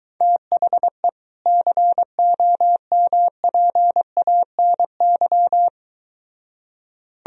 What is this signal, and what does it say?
Morse code, heard as a keyed tone, about 23 wpm, 700 Hz. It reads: THE COMPANY